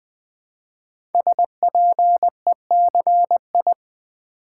Morse code, heard as a keyed tone, about 20 wpm, 700 Hz. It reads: SPECI
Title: Build A Word - 5 Letters At A Time - 12-Letter Words - 20wpm